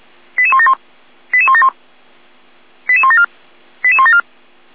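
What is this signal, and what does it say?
Five-tone selective call (ZVEI), heard as quick runs of five tones, tones of about 70 ms. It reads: 88171 88171 88174 88174
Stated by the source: ZVEI five-tone call received by SDR.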